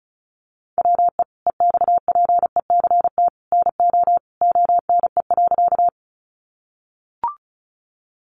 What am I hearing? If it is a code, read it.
Morse code, heard as a keyed tone, about 34 wpm, 700 Hz. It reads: WE EXPECT NO ONE.
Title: Sentences from Top 1100 Words 35wpm